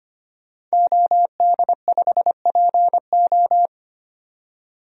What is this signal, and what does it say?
Morse code, heard as a keyed tone, about 25 wpm, 700 Hz. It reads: OD5PO
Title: DXCC-100 Call Signs - 5 Characters - 25wpm